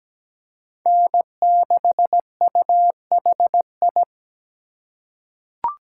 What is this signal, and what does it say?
Morse code, heard as a keyed tone, about 17 wpm, 700 Hz. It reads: N6UHI
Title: USA Call Signs - 5 Characters - 2x Character Spacing - 17wpm